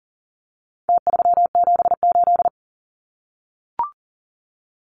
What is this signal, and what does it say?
Morse code, heard as a keyed tone, about 40 wpm, 700 Hz. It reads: T378
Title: Numbers - Contest Style - 4-Digit 40wpm